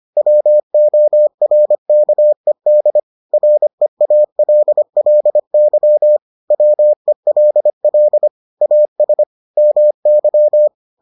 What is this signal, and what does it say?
Morse code, heard as a keyed tone, about 25 wpm, 600 Hz. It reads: WORKED REALLY WELL AS MY